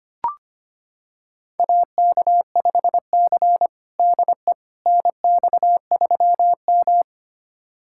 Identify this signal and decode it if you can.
Morse code, heard as a keyed tone, about 25 wpm, 700 Hz. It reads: AK5C DE NX3M
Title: USA Callsign DE Callsign - 4 Characters - 25wpm